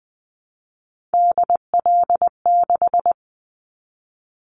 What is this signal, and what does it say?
Morse code, heard as a keyed tone, about 20 wpm, 700 Hz. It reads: DL6
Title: Callsign Prefixes for Top 25 DXCC - 20wpm